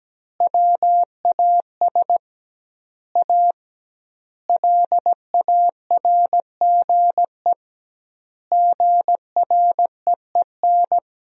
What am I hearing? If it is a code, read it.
Morse code, heard as a keyed tone, about 17 wpm, 700 Hz. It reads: WAS A LARGE GREEN
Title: 2x Word-Spacing - Sentences from Top 500 Words - 17wpm